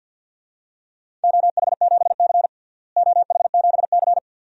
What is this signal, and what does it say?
Morse code, heard as a keyed tone, about 52 wpm, 700 Hz. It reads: OH7C OH7C
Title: DXCC-100 Call Signs - 4 Characters - Repeated 2x - 50wpm